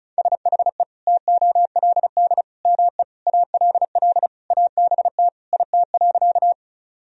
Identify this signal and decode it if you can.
Morse code, heard as a keyed tone, about 36 wpm, 700 Hz. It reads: SHE TOLD ME ALL ABT IT.
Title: Sentences from Top 400 Words and Abbreviations 35wpm